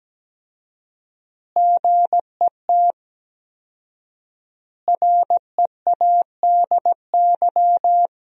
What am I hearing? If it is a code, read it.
Morse code, heard as a keyed tone, about 17 wpm, 700 Hz. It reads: GET READY
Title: 4x Word-Spacing - Sentences from Top 400 Words - 17wpm